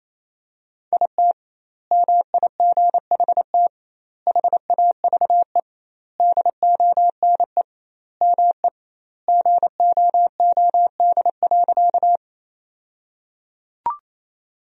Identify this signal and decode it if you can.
Morse code, heard as a keyed tone, about 28 wpm, 700 Hz. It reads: IT MIGHT HAVE DONE ME GOOD.
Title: 2x Word-Spacing - Sentences from Top 500 Words - 28wpm